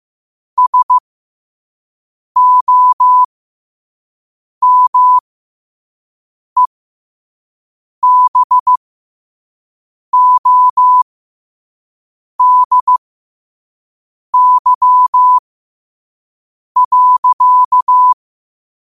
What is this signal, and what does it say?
Morse code, heard as a keyed tone, about 15 wpm, spaced as for 5 wpm, 1000 Hz. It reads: SOMEBODY.
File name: qotdfiles / 5wpm